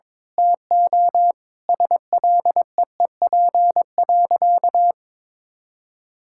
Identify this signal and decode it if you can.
Morse code, heard as a keyed tone, about 22 wpm, 700 Hz. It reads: TO SLEEP.
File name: Sentences with 7-Word Phrases - 22wpm